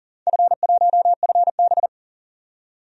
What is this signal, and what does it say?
Morse code, heard as a keyed tone, about 40 wpm, 700 Hz. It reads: F1FB
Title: DXCC-100 Call Signs - 4 Characters - 40wpm